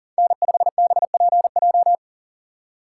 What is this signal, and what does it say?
Morse code, heard as a keyed tone, about 40 wpm, 700 Hz. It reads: N5BPJ
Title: USA Call Signs - 5 Characters - 2x Character Spacing - 40wpm